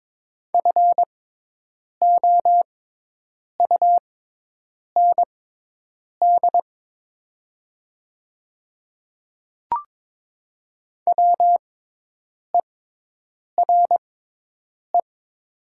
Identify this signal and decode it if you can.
Morse code, heard as a keyed tone, about 22 wpm, spaced as for 8 wpm, 700 Hz. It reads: FOUND WERE
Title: 6x Character Spacing - Top 500 Words - 22wpm